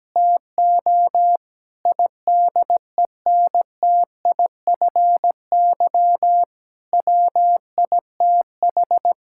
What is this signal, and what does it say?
Morse code, heard as a keyed tone, about 17 wpm, 700 Hz. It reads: TO IDENTIFY WITH
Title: Sentences from Top 1000 Words - Review Part 6 - 17wpm